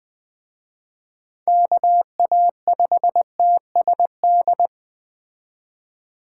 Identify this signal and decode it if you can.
Morse code, heard as a keyed tone, about 21 wpm, 700 Hz. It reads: KA5TSD